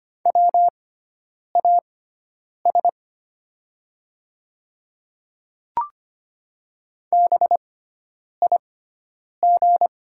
Morse code, heard as a keyed tone, about 25 wpm, spaced as for 9 wpm, 700 Hz. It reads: WAS BIG
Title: 6x Character Spacing - Top 200 Words - 25wpm